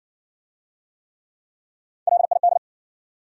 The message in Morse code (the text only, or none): LID